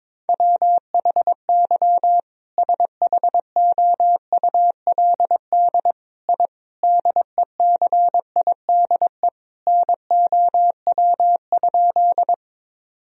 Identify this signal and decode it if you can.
Morse code, heard as a keyed tone, about 22 wpm, 700 Hz. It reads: WHY SHOULD I DECIDE NOW?
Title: Sentences from Top 1000 Words - Review Part 3 - 22wpm